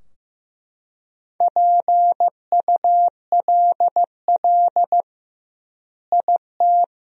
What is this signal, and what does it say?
Morse code, heard as a keyed tone, about 15 wpm, 700 Hz. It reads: PULL IT
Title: 2x Word-Spacing - Sentences from Top 400 Words - 15wpm